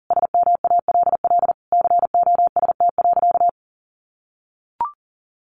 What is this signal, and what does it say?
Morse code, heard as a keyed tone, about 40 wpm, 700 Hz. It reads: SMALL COST.
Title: Sentences from Top 900 Words 40wpm